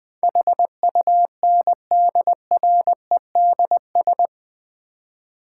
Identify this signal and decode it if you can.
Morse code, heard as a keyed tone, about 21 wpm, 700 Hz. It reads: HUNDREDS
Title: 4x Word-Spacing - Sets of 4 Words - 20wpm